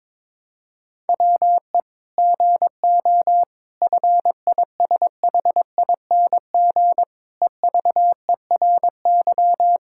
Morse code, heard as a keyed tone, about 22 wpm, 700 Hz. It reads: WE GO FISHING EVERY
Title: Sentences from Top 1000 Words - Review Part 5 - 22wpm